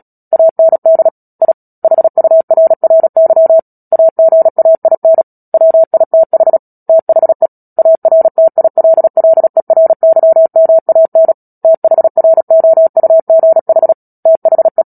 Morse code, heard as a keyed tone, about 37 wpm, 650 Hz. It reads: AND I HURRY AGAIN WITH THE ARTILLERYMAN THROUGH THE
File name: Book Two / 10-Chapter 10 - Epilogue - 36 WPM